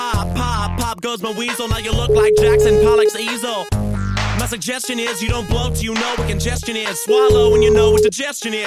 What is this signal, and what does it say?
Ringback tone (Europe, 425 Hz), with speech.